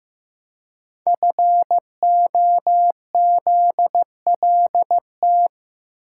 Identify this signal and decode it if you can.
Morse code, heard as a keyed tone, about 15 wpm, 700 Hz. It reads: FOZLT
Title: Code Groups - 5 Letters - 15wpm